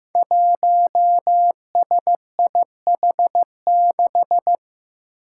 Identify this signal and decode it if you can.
Morse code, heard as a keyed tone, about 15 wpm, 700 Hz. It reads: 1SIH6
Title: Five Challenging Letter-Number Combinations 15wpm